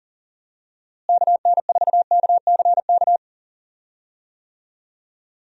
Morse code, heard as a keyed tone, about 40 wpm, 700 Hz. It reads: KN4KCK
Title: USA Call Signs - 6 Characters - Repeated 2x - 40wpm